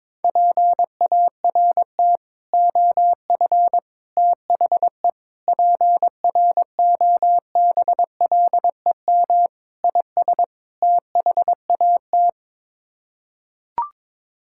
Morse code, heard as a keyed tone, about 22 wpm, 700 Hz. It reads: PART OF THE PROBLEM IS THAT
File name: practice / 22wpm / Sets of 6 Words - 22wpm